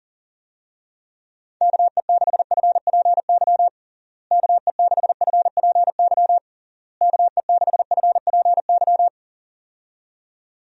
Morse code, heard as a keyed tone, about 37 wpm, 700 Hz. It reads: KE6FPY KE6FPY KE6FPY
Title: USA Call Signs - 6 Characters - Repeated 3x - 40wpm